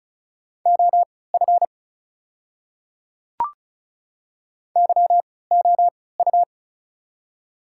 Morse code, heard as a keyed tone, about 36 wpm, spaced as for 20 wpm, 700 Hz. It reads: OF YOU
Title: 3x Character Spacing - Top 100 Words - 35wpm